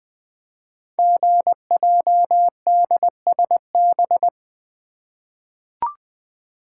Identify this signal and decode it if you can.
Morse code, heard as a keyed tone, about 20 wpm, 700 Hz. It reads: GJDSB
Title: Code Groups - 5 Letters - 20wpm